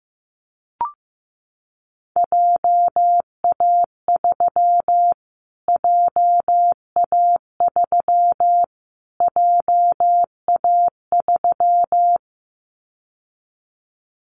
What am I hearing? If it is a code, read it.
Morse code, heard as a keyed tone, about 15 wpm, 700 Hz. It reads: JA3 JA3 JA3
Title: Callsign Prefixes for Top 25 DXCC - Repeated 3x - 15wpm